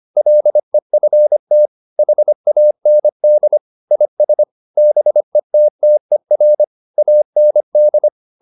Morse code, heard as a keyed tone, about 25 wpm, 600 Hz. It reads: LEFT HAND IS BETTER AND